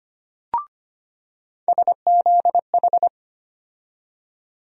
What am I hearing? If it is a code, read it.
Morse code, heard as a keyed tone, about 25 wpm, 700 Hz. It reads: SZH
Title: USA Call Sign Postfixes - 3 Letters 25wpm